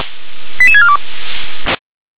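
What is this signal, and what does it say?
Five-tone selective call (ZVEI), heard as quick runs of five tones, tones of about 70 ms. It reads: 88642
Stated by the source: ZVEI five-tone selective call received by SDR.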